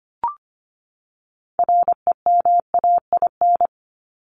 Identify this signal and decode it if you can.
Morse code, heard as a keyed tone, about 25 wpm, 700 Hz. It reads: REMAIN
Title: ING Suffix - Top 100 Words with Base Words 25wpm